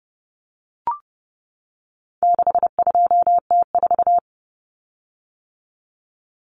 Morse code, heard as a keyed tone, about 30 wpm, 700 Hz. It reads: 62T4
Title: Common Cut Numbers - 4-Digit 30wpm